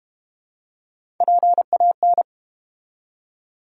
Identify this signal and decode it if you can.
Morse code, heard as a keyed tone, about 30 wpm, 700 Hz. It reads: PAN
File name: Easy Words - Speed-Racing with Hints - 25wpm